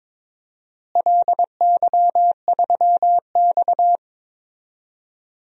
Morse code, heard as a keyed tone, about 21 wpm, 700 Hz. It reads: LY3X
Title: DXCC-100 Call Signs - 4 Characters - Repeated 3x - 22wpm